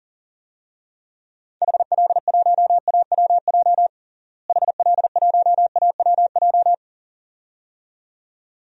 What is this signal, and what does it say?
Morse code, heard as a keyed tone, about 40 wpm, 700 Hz. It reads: HL1AWJ HL1AWJ
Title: DXCC-100 Call Signs - 6 Characters - Repeated 2x - 40wpm